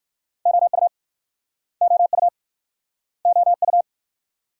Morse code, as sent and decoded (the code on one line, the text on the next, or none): --- ..- / --- ..- / --- ..-
OU OU OU